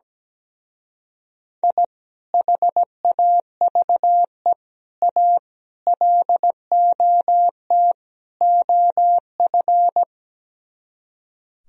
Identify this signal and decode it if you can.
Morse code, heard as a keyed tone, about 17 wpm, 700 Hz. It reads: I HAVE A LOT OF
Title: Speed-Racing - Sets of 5 Words 17wpm